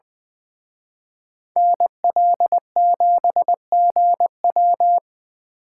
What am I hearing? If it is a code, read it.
Morse code, heard as a keyed tone, about 24 wpm, 700 Hz. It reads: NL7GW
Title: Speed Racing - USA 2x2 Call Signs 20wpm